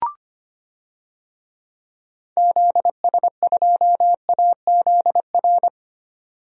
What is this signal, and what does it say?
Morse code, heard as a keyed tone, about 25 wpm, 700 Hz. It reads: ZS2AZR
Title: DXCC-100 Call Signs - 6 Characters - 25wpm